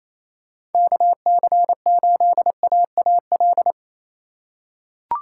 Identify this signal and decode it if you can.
Morse code, heard as a keyed tone, about 28 wpm, 700 Hz. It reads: KC8AAL
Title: USA Call Signs - 4x Character Spacing - 28wpm